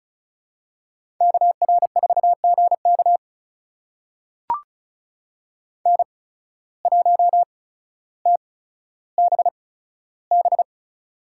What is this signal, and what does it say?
Morse code, heard as a keyed tone, about 34 wpm, spaced as for 15 wpm, 700 Hz. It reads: KR4GK N1TBB
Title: USA Call Signs - 5 Characters - 8x Character Spacing - 35wpm